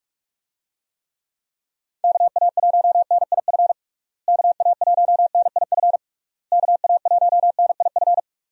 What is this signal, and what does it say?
Morse code, heard as a keyed tone, about 42 wpm, 700 Hz. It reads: KA1NIF KA1NIF KA1NIF
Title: USA Call Signs - 6 Characters - Repeated 3x - 45wpm